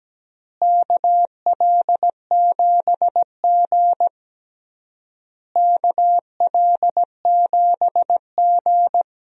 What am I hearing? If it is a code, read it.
Morse code, heard as a keyed tone, about 17 wpm, 700 Hz. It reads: KL7G KL7G
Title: DXCC-100 Call Signs - 4 Characters - Repeated 3x - 17wpm